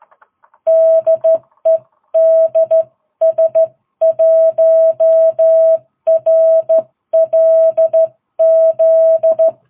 Morse code, heard as a keyed tone, about 13 wpm, 660 Hz. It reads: DE DS1RLZ
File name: jh8cbh / 2022cw